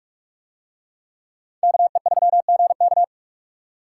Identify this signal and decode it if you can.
Morse code, heard as a keyed tone, about 49 wpm, 700 Hz. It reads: KE3GK